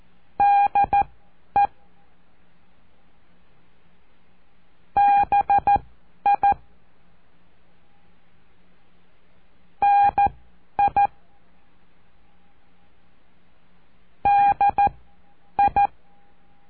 Morse code, heard as a keyed tone, about 14 wpm, spaced as for 9 wpm, 810 Hz. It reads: DE BI NI DI
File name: cursos / CURSO DE TELEGRAFIA / Telegrafia - Trilha 05